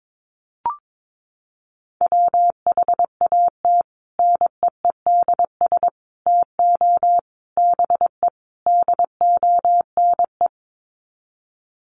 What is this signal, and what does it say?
Morse code, heard as a keyed tone, about 22 wpm, 700 Hz. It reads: WHAT NEEDS TO BE DONE
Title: Sets of 5 Words Encore 22wpm